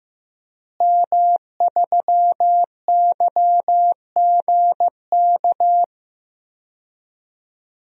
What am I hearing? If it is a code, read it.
Morse code, heard as a keyed tone, about 15 wpm, 700 Hz. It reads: M3YGK
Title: DXCC-100 Call Signs - 5 Characters - Repeated 2x - 15wpm